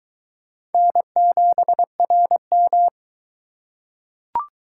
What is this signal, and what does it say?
Morse code, heard as a keyed tone, about 22 wpm, 700 Hz. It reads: N7RM